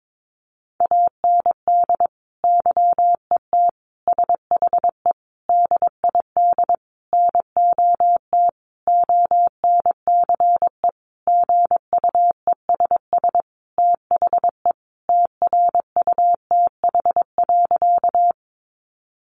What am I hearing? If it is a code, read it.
Morse code, heard as a keyed tone, about 22 wpm, 700 Hz. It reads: AND YET SHE DID NOT ONCE GUESS THE TRUTH.
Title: Sentences from Top 1500 Words 22wpm